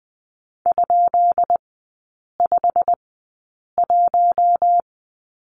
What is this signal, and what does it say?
Morse code, heard as a keyed tone, about 20 wpm, spaced as for 8 wpm, 700 Hz. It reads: ?51
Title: 218 Lesson 35e - Number 0 - Three Characters 20wpm